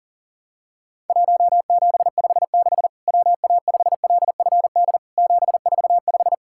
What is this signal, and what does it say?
Morse code, heard as a keyed tone, about 40 wpm, 700 Hz. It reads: 1756 WA5LFD 745
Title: Worked All Europe DX Contest - QTC Exchange - 40wpm